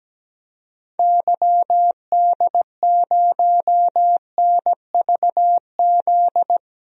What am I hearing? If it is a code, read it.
Morse code, heard as a keyed tone, about 17 wpm, 700 Hz. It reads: YD0NVZ
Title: DXCC-100 Call Signs - 6 Characters - Repeated 3x - 17wpm